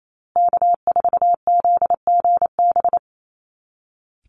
Morse code, heard as a keyed tone, about 28 wpm, 700 Hz. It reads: K4ZGB
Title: Call Signs Repeated 2x - 28wpm